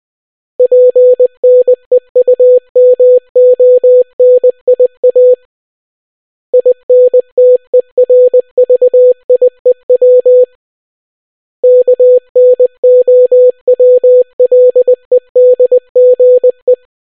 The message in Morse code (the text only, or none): PNEUMONIA INTERVIEW KNOWLEDGE